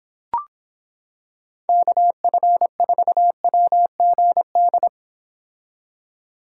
Morse code, heard as a keyed tone, about 26 wpm, 700 Hz. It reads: KF4WGD